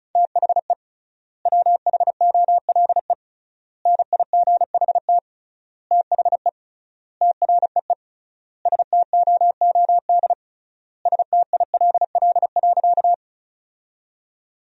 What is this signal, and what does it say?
Morse code, heard as a keyed tone, about 34 wpm, 700 Hz. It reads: THE WHOLE NIGHT THE TREE STOOD STILL.
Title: 3x Word-Spacing - Sentences from Top 500 Words - 35wpm